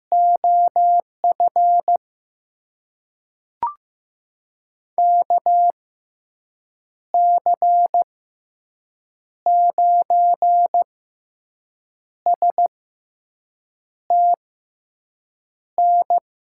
Morse code, heard as a keyed tone, about 15 wpm, spaced as for 6 wpm, 700 Hz. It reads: OF KC9STN
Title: USA Call Signs - 6x Character Spacing - 15wpm